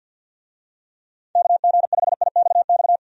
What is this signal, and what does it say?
Morse code, heard as a keyed tone, about 51 wpm, 700 Hz. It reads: KG5IXX
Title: USA Call Signs - 6 Characters - 50wpm